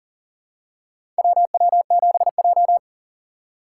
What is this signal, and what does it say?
Morse code, heard as a keyed tone, about 40 wpm, 700 Hz. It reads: WW7J
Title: USA Call Signs - 4 Characters - 4x Character Spacing - 40wpm